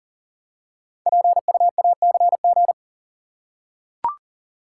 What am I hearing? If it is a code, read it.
Morse code, heard as a keyed tone, about 40 wpm, 690 Hz. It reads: PUACG
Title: Code Groups - 5 Letters - 40wpm